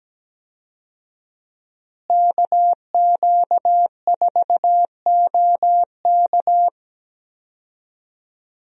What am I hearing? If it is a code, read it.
Morse code, heard as a keyed tone, about 17 wpm, 700 Hz. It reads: KQ4OK
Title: USA Call Signs - 5 Characters - Repeated 3x - 17wpm